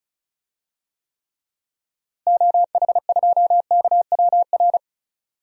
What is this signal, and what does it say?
Morse code, heard as a keyed tone, about 36 wpm, 700 Hz. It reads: OH2KWR
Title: DXCC-100 Call Signs - 6 Characters - Repeated 2x - 35wpm